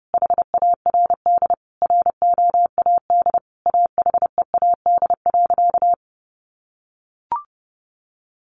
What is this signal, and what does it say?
Morse code, heard as a keyed tone, about 28 wpm, 700 Hz. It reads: HARD ROAD AHEAD.
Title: Sentences from Top 1100 Words 30wpm